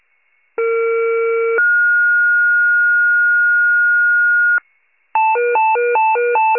Two-tone page, heard as a one-second tone, then a three-second tone, about 460 then 1430 Hz. It is heard from 0.6 to 4.6 s.
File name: pager / 21 FIRE / wcalert